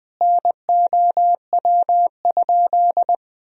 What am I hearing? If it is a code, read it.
Morse code, heard as a keyed tone, about 20 wpm, 700 Hz. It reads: NOW?
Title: Sentences from Top 200 Words and Abbreviations 20wpm